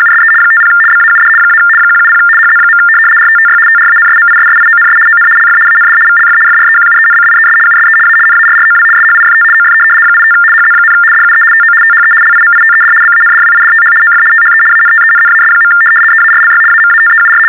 RTTY, about 75 baud. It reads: USE 50 BAUD - 170 HZ. RTTY TRANSMISSIONS ON LF AND VLF FREQUENCIES USE A NARROW SHIFT OF 85 HZ, DUE TO THE LIMITED BANDWIDTH OF THE ANTENNAS. EARLY AMATEUR
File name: teletype-RTTY-75